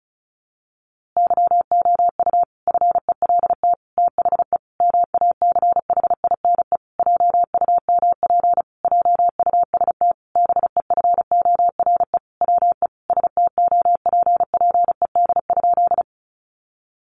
Morse code, heard as a keyed tone, about 34 wpm, 700 Hz. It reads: YOU FELT THE MACHINE JUMP JUST BEFORE WE STOPPED?
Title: A Sound Of Thunder 35wpm